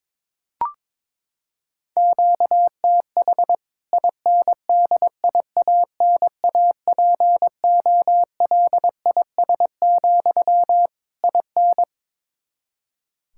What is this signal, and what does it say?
Morse code, heard as a keyed tone, about 22 wpm, 700 Hz. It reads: QTH INDIANAPOLIS, IN
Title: K1USN SST QSO Practice 22wpm